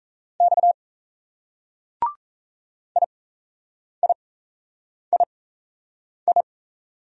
Morse code, heard as a keyed tone, about 34 wpm, spaced as for 8 wpm, 700 Hz. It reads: X IIII
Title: Speed-Racing Single Letters 28wpm